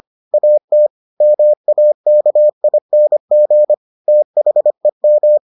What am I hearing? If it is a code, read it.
Morse code, heard as a keyed tone, about 25 wpm, 600 Hz. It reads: AT MAKING THEM